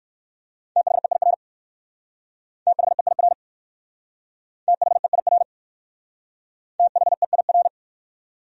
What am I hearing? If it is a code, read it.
Morse code, heard as a keyed tone, about 59 wpm, 700 Hz. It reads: THEIR THEIR THEIR THEIR